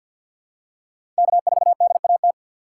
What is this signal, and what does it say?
Morse code, heard as a keyed tone, about 49 wpm, 700 Hz. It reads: K4DAT
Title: USA Call Signs - 5 Characters - 4x Character Spacing - 50wpm